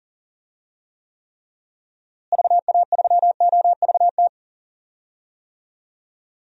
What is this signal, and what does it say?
Morse code, heard as a keyed tone, about 40 wpm, 700 Hz. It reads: VA3OVT
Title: DXCC-100 Call Signs - 6 Characters - Repeated 3x - 40wpm